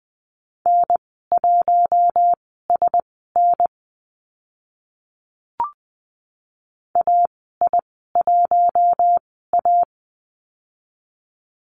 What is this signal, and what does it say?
Morse code, heard as a keyed tone, about 20 wpm, spaced as for 14 wpm, 700 Hz. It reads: N1SN AI1A
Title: 062 Lesson 11g - Letter h - Call Signs 20wpm